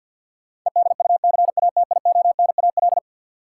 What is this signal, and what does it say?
Morse code, heard as a keyed tone, about 53 wpm, 700 Hz. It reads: EDUCATIONAL
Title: Sentences from Top 1200 Words - New Word Drill - 50wpm